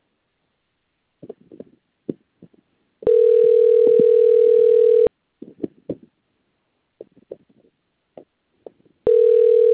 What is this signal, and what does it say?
Ringback tone (North America), 440 + 480 Hz.